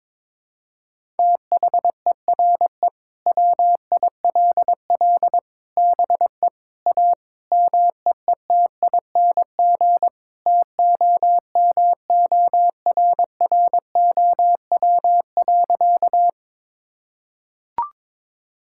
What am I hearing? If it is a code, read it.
Morse code, heard as a keyed tone, about 22 wpm, 700 Hz. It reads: THERE WILL BE A MEETING TOMORROW.
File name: practice / 22wpm / Sentences with 4-Word Phrases - 22wpm